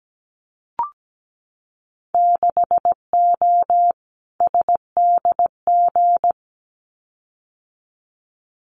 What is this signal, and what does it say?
Morse code, heard as a keyed tone, about 17 wpm, 700 Hz. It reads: 6O SDG
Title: Winter Field Day Exchanges Encore 17wpm